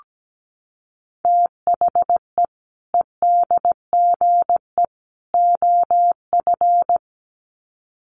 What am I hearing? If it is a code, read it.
Morse code, heard as a keyed tone, about 17 wpm, 700 Hz. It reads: THE EDGE OF
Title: Sets of 3 Words 17wpm